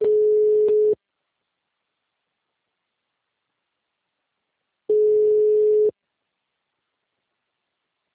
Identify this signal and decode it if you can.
Ringback tone (Europe, 425 Hz).